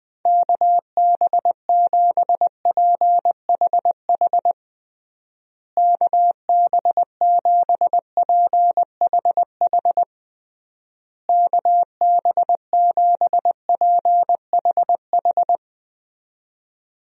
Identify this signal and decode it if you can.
Morse code, heard as a keyed tone, about 20 wpm, 700 Hz. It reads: KB7PHH KB7PHH KB7PHH
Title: USA Call Signs - 6 Characters - Repeated 3x - 20wpm